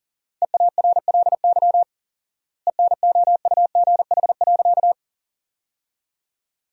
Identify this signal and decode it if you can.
Morse code, heard as a keyed tone, about 40 wpm, 700 Hz. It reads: EARLY ENOUGH.
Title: 4x Word-Spacing - Sentences from Top 400 Words - 40wpm